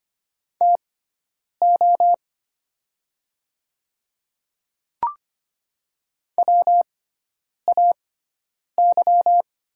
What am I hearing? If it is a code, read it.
Morse code, heard as a keyed tone, about 25 wpm, spaced as for 9 wpm, 700 Hz. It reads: TO WAY
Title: 6x Character Spacing - Top 100 Words - 25wpm